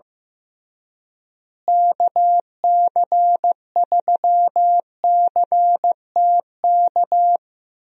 Morse code, heard as a keyed tone, about 14 wpm, 700 Hz. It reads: KC3CTK
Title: USA Call Signs - 6 Characters - Repeated 3x - 15wpm